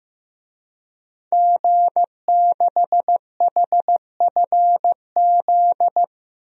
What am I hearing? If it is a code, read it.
Morse code, heard as a keyed tone, about 15 wpm, 700 Hz. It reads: G6HFZ
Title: DXCC-100 Call Signs - 5 Characters - 15wpm